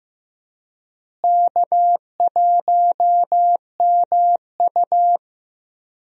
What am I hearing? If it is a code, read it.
Morse code, heard as a keyed tone, about 15 wpm, 700 Hz. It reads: K1MU
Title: USA Call Signs - 4 Characters - Repeated 3x - 15wpm